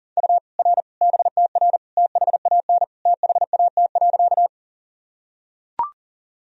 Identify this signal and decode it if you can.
Morse code, heard as a keyed tone, about 40 wpm, 700 Hz. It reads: U R BTR THAN THAT.
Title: Sentences from Top 400 Words and Abbreviations 40wpm